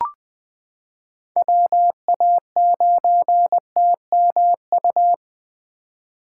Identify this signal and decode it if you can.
Morse code, heard as a keyed tone, about 20 wpm, 700 Hz. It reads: WA9TMU
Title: 2023 POTA Hunter Call Signs - 20wpm